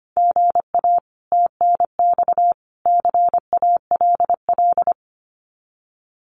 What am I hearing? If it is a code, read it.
Morse code, heard as a keyed tone, about 24 wpm, 700 Hz. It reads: GA TNX CALL
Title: Common QSO Phrases 25wpm